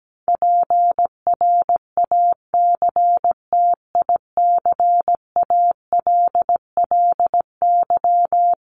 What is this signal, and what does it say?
Morse code, heard as a keyed tone, about 17 wpm, 700 Hz. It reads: PRACTICALLY
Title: LY Suffix - Top 100 Words with Base Words 17wpm